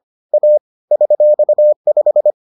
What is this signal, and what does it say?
Morse code, heard as a keyed tone, about 25 wpm, 600 Hz. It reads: A $5